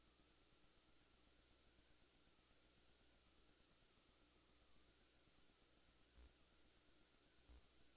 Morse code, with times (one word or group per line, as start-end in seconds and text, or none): none